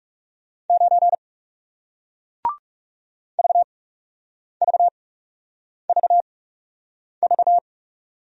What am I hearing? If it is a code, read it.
Morse code, heard as a keyed tone, about 40 wpm, spaced as for 8 wpm, 700 Hz. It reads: 9 VVVV